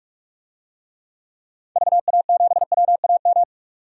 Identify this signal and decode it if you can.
Morse code, heard as a keyed tone, about 46 wpm, 700 Hz. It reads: VA7WAM
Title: DXCC-100 Call Signs - 6 Characters - 45wpm